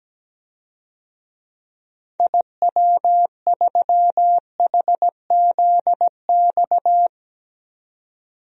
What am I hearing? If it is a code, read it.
Morse code, heard as a keyed tone, about 17 wpm, 700 Hz. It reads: IW3HZX
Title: DXCC-100 Call Signs - 6 Characters - 17wpm